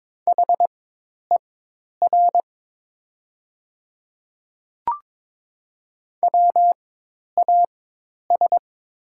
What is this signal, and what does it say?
Morse code, heard as a keyed tone, about 22 wpm, spaced as for 10 wpm, 700 Hz. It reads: HER WAS
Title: 4x Character Spacing - Top 100 Words - 22wpm